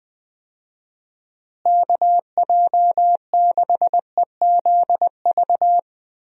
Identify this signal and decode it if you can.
Morse code, heard as a keyed tone, about 20 wpm, 700 Hz. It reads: KJ6EZV
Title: USA Call Signs - 6 Characters - Repeated 2x - 20wpm